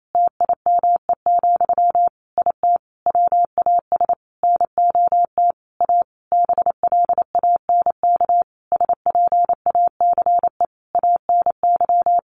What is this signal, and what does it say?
Morse code, heard as a keyed tone, about 28 wpm, 700 Hz. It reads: TIME, IT WAS NOT A BLANK SPACE ANY
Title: Sentences from Top 1500 Words 28wpm